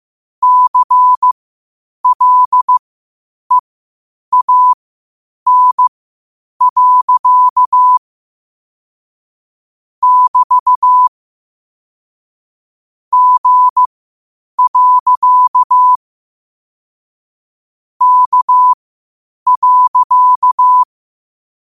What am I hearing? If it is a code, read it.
Morse code, heard as a keyed tone, about 15 wpm, spaced as for 8 wpm, 1000 Hz. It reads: CLEAN. <BT> G. K.
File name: qotdfiles / 8wpm